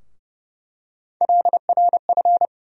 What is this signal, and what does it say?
Morse code, heard as a keyed tone, about 30 wpm, 700 Hz. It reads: LRF
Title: USA Call Sign Postfixes - 3 Letters 30wpm